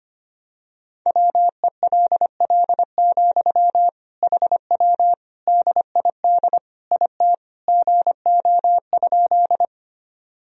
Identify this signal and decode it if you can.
Morse code, heard as a keyed tone, about 25 wpm, 700 Hz. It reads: WELL, HW DID IT GO?
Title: Sentences from Top 200 Words and Abbreviations 25wpm